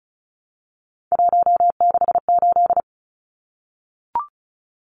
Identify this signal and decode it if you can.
Morse code, heard as a keyed tone, about 34 wpm, 700 Hz. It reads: 168
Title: Numbers - 3-Digit 35wpm